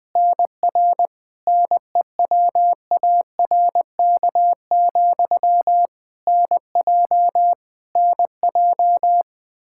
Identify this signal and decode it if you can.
Morse code, heard as a keyed tone, about 20 wpm, 700 Hz. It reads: NR NEWARK, NJ NJ
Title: K1USN SST QSO Practice 20wpm